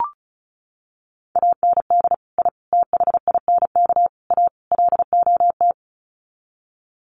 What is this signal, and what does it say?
Morse code, heard as a keyed tone, about 34 wpm, 700 Hz. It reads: AND I THINK A LOT
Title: Sets of 5 Words Encore2 35wpm